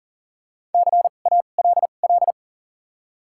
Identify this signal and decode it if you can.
Morse code, heard as a keyed tone, about 40 wpm, spaced as for 29 wpm, 700 Hz. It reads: CALL
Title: 2x Character Spacing - Top 100 Words - 40wpm